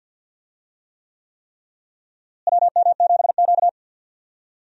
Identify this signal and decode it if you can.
Morse code, heard as a keyed tone, about 47 wpm, 700 Hz. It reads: WM7Q
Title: USA Call Signs - 4 Characters - Repeated 3x - 50wpm